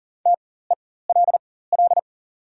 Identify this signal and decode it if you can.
Morse code, heard as a keyed tone, about 39 wpm, spaced as for 19 wpm, 700 Hz. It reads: TELL
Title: 4x Character Spacing - Top 300 Words - 40wpm